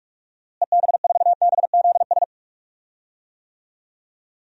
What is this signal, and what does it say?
Morse code, heard as a keyed tone, about 46 wpm, 700 Hz. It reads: EB4BZS
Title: DXCC-100 Call Signs - 6 Characters - Repeated 3x - 45wpm